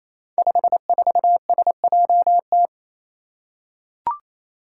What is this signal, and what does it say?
Morse code, heard as a keyed tone, about 28 wpm, 700 Hz. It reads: 54SJT
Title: Five Challenging Letter-Number Combinations 28wpm